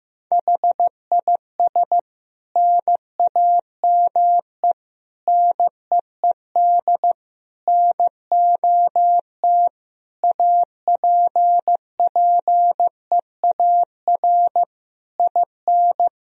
Morse code, heard as a keyed tone, about 16 wpm, 700 Hz. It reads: HIS NAME NEED NOT APPEAR IN
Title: Sentences from Top 1000 Words - Review Part 2 - 15wpm